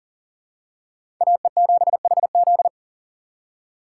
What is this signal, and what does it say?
Morse code, heard as a keyed tone, about 40 wpm, 700 Hz. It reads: AE7HZ
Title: USA Call Signs - 5 Characters - 6x Character Spacing - 40wpm